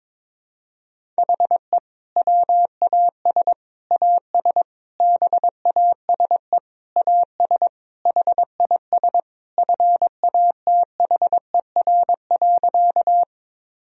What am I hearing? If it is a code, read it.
Morse code, heard as a keyed tone, about 22 wpm, 700 Hz. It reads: HE WAS AS BASE AS HIS FATHER.
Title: Sentences from Top 1000 Words - Review Part 1 - 22wpm